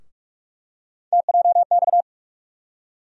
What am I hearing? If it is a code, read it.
Morse code, heard as a keyed tone, about 44 wpm, 700 Hz. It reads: TJX
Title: USA Call Sign Postfixes - 3 Letters 45wpm